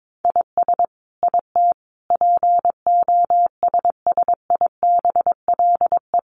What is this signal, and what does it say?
Morse code, heard as a keyed tone, about 22 wpm, 700 Hz. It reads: IS IT POSSIBLE
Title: Sentences from Top 1200 Words 22wpm